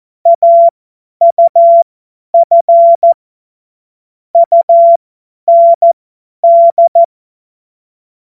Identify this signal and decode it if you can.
Morse code, heard as a keyed tone, about 14 wpm, spaced as for 10 wpm, 680 Hz. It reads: AUF UND
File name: Morsekurs / Witze / Witz 19. Oktober 2021, 50 F70